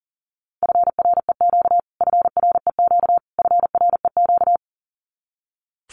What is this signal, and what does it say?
Morse code, heard as a keyed tone, about 40 wpm, 700 Hz. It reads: FREQ FREQ FREQ
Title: Abbreviations Repeated 3x - 40wpm